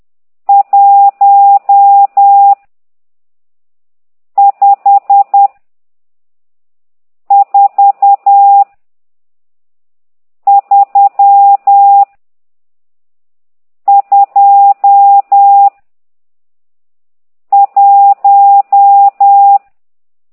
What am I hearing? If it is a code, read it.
Morse code, heard as a keyed tone, about 10 wpm, 800 Hz. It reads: 1 5 4 3 2 1
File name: ct2idw / exerc12